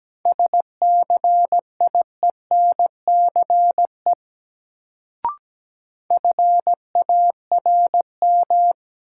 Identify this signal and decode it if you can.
Morse code, heard as a keyed tone, about 17 wpm, 700 Hz. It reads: SCIENCE FARM
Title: Sentences from Top 1000 Words - Review Part 2 - New Word Drill - 17wpm